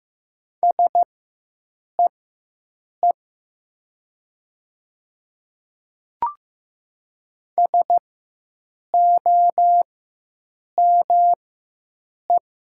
Morse code, heard as a keyed tone, about 15 wpm, spaced as for 7 wpm, 700 Hz. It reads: SEE SOME
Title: 4x Character Spacing - Top 100 Words - 15wpm